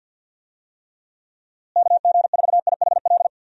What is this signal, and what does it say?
Morse code, heard as a keyed tone, about 51 wpm, 700 Hz. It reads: KG4IHL
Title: USA Call Signs - 6 Characters - Repeated 2x - 50wpm